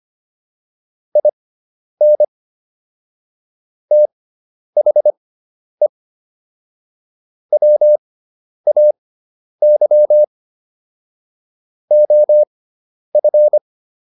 Morse code, heard as a keyed tone, about 25 wpm, spaced as for 10 wpm, 600 Hz. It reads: IN THE WAY OF